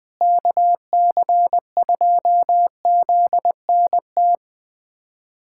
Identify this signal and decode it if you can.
Morse code, heard as a keyed tone, about 21 wpm, 700 Hz. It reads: KC2ZNT
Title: USA Callsign DE Callsign 20wpm